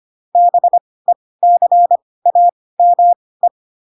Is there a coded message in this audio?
Yes: Morse code from 0.4 to 3.5 s, heard as a keyed tone, about 25 wpm, spaced as for 18 wpm, 700 Hz. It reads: BECAME